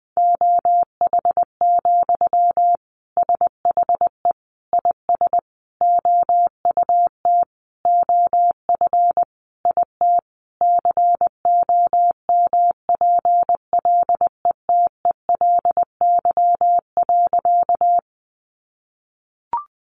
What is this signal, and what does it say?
Morse code, heard as a keyed tone, about 20 wpm, 700 Hz. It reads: OH, SHE IS OUT OF IT COMPLETELY.
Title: Sentences from Top 1100 Words 20wpm